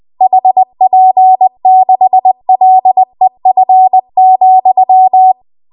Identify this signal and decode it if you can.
Morse code, heard as a keyed tone, about 20 wpm, 750 Hz. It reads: HP6LEF,
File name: Archive / 20 WPM / 20WPM